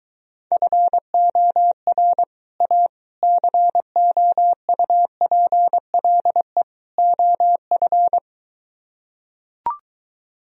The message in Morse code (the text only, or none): FOR A COUPLE OF